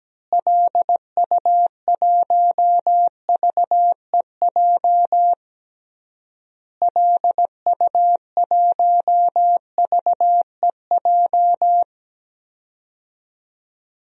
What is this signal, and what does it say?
Morse code, heard as a keyed tone, about 17 wpm, 700 Hz. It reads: LU1VEJ LU1VEJ
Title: DXCC-100 Call Signs - 6 Characters - Repeated 3x - 17wpm